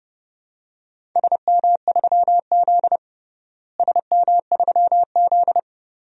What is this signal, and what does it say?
Morse code, heard as a keyed tone, about 32 wpm, 700 Hz. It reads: SM3Z SM3Z
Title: DXCC-100 Call Signs - 4 Characters - Repeated 2x - 30wpm